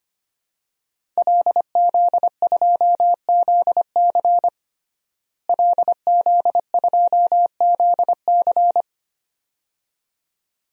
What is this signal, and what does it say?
Morse code, heard as a keyed tone, about 25 wpm, 700 Hz. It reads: LZ2ZC LZ2ZC
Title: DXCC-100 Call Signs - 5 Characters - Repeated 2x - 25wpm